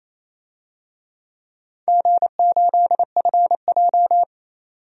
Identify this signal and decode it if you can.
Morse code, heard as a keyed tone, about 28 wpm, 700 Hz. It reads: G8FJ
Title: DXCC-100 Call Signs - 4 Characters - 28wpm